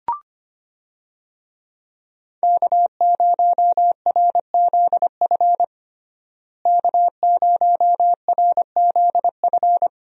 Morse code, heard as a keyed tone, about 25 wpm, 700 Hz. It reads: K0RZF K0RZF
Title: USA Call Signs - 5 Characters - Repeated 2x - 25wpm